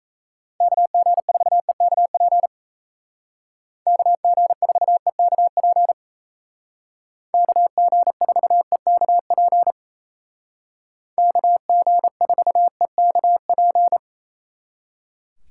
Morse code, heard as a keyed tone, about 36 wpm, 700 Hz. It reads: KG4EKP KG4EKP KG4EKP KG4EKP